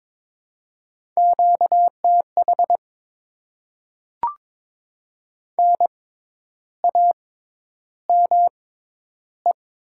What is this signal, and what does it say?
Morse code, heard as a keyed tone, about 22 wpm, spaced as for 10 wpm, 700 Hz. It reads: QTH NAME
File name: Top 20 QSO Elements - 6x Character Spacing - 22wpm